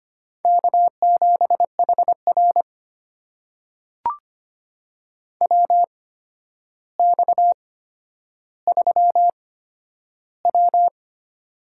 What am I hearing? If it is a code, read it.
Morse code, heard as a keyed tone, about 25 wpm, spaced as for 11 wpm, 700 Hz. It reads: K7HR WX3W
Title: USA Call Signs - 4 Characters - 8x Character Spacing - 25wpm